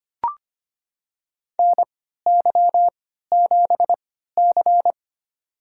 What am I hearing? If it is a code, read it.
Morse code, heard as a keyed tone, about 25 wpm, spaced as for 14 wpm, 700 Hz. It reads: NY7C
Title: USA Call Signs - 4 Characters - 3x Character Spacing - 25wpm